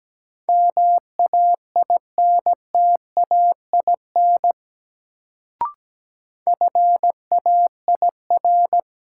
Morse code, heard as a keyed tone, about 17 wpm, 700 Hz. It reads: MAINTAIN FAIR
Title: Sentences from Top 1000 Words - Review Part 5 - New Word Drill - 17wpm